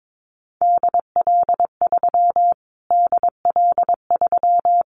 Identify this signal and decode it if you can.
Morse code, heard as a keyed tone, about 22 wpm, 700 Hz. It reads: DL3 DL3
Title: Callsign Prefixes for Top 25 DXCC - Repeated 3x - 22wpm